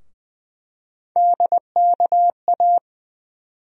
Morse code, heard as a keyed tone, about 20 wpm, 700 Hz. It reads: DKA
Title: USA Call Sign Postfixes - 3 Letters 20wpm